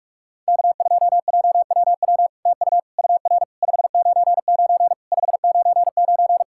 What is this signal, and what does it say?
Morse code, heard as a keyed tone, about 46 wpm, 700 Hz. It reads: K2JWW TU UR 599 599
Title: SOTA QSO Practice 45wpm